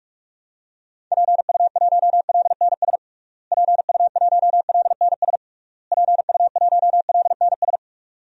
Morse code, heard as a keyed tone, about 46 wpm, 700 Hz. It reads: PU1LNS PU1LNS PU1LNS
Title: DXCC-100 Call Signs - 6 Characters - Repeated 3x - 45wpm